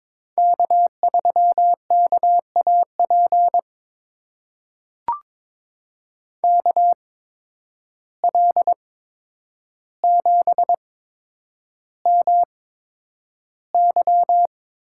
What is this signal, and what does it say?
Morse code, heard as a keyed tone, about 22 wpm, spaced as for 9 wpm, 700 Hz. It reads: K3KAP KL7MY
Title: USA Call Signs - 5 Characters - 8x Character Spacing - 22wpm